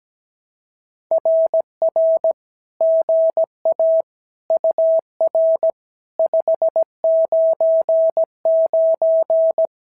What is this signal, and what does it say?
Morse code, heard as a keyed tone, about 17 wpm, 650 Hz. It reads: RR GA UR 599
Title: SOTA QSO Practice 17wpm